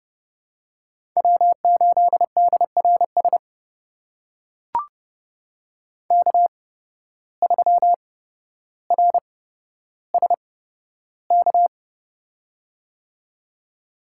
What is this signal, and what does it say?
Morse code, heard as a keyed tone, about 30 wpm, spaced as for 13 wpm, 700 Hz. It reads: W8DRS K3RSK
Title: USA Call Signs - 5 Characters - 8x Character Spacing - 30wpm